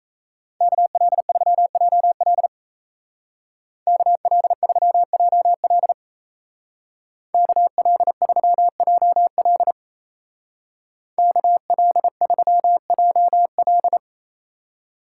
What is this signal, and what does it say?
Morse code, heard as a keyed tone, about 33 wpm, 700 Hz. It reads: KL3JL KL3JL KL3JL KL3JL
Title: Speed Racing - USA 2x2 Call Signs 28wpm